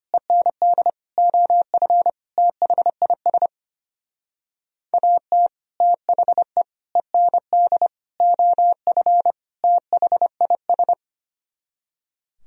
Morse code, heard as a keyed tone, about 32 wpm, 700 Hz. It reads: END OF THIS AT THE END OF THIS